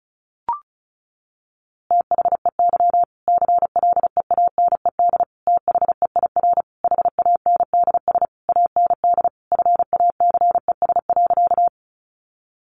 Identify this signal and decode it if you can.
Morse code, heard as a keyed tone, about 34 wpm, 700 Hz. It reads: THEY CLEANED THEIR HANDS AND FACES.
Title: A Sound Of Thunder 35wpm